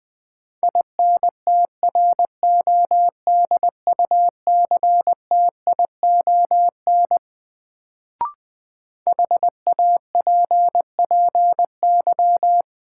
Morse code, heard as a keyed tone, about 20 wpm, 700 Hz. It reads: INTRODUCTION HAPPY
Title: Sentences from Top 1000 Words - New Word Drill - 20wpm